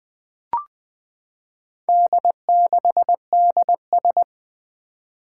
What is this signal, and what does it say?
Morse code, heard as a keyed tone, about 20 wpm, 700 Hz. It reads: D6DS
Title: Four Challenging Letter-Number Combinations 20wpm